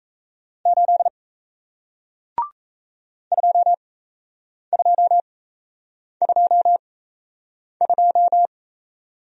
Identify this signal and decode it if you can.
Morse code, heard as a keyed tone, about 37 wpm, spaced as for 8 wpm, 700 Hz. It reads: 8 2222